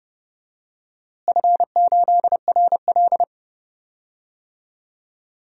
Morse code, heard as a keyed tone, about 30 wpm, 700 Hz. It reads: F8RL
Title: DXCC-100 Call Signs - 4 Characters - Repeated 2x - 30wpm